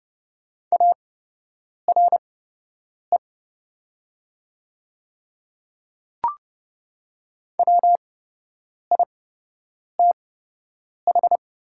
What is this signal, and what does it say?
Morse code, heard as a keyed tone, about 30 wpm, spaced as for 8 wpm, 700 Hz. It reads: ARE WITH
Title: 8x Character Spacing - Top 100 Words - 30wpm